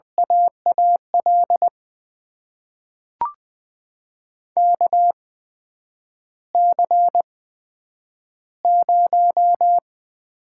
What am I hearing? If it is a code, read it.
Morse code, heard as a keyed tone, about 21 wpm, spaced as for 9 wpm, 700 Hz. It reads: AAL KC0
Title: USA Call Signs - 6 Characters - 8x Character Spacing - 20wpm